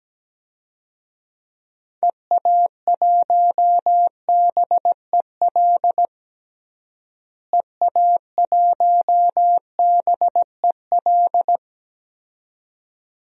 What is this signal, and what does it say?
Morse code, heard as a keyed tone, about 17 wpm, 700 Hz. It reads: EA1BEL EA1BEL
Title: DXCC-100 Call Signs - 6 Characters - Repeated 2x - 17wpm